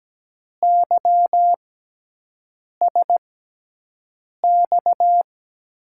Morse code, heard as a keyed tone, about 17 wpm, spaced as for 6 wpm, 700 Hz. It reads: YSX
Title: USA Call Signs - 6 Characters - 6x Character Spacing - 17wpm